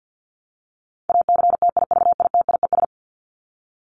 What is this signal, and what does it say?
Morse code, heard as a keyed tone, about 48 wpm, 700 Hz. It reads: ACTIVITIES
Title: ES:S Suffix - Top 100 Words with Base Words 50wpm